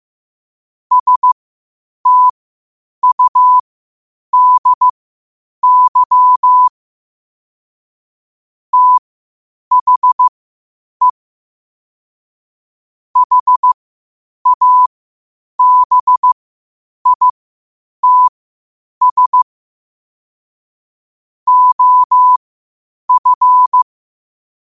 Morse code, heard as a keyed tone, about 15 wpm, spaced as for 8 wpm, 1000 Hz. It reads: STUDY THE HABITS OF
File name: qotdfiles / 8wpm